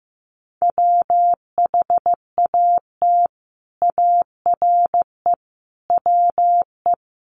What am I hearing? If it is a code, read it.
Morse code, heard as a keyed tone, about 15 wpm, 700 Hz. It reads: WHAT ARE WE
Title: Sets of 5 Words 15wpm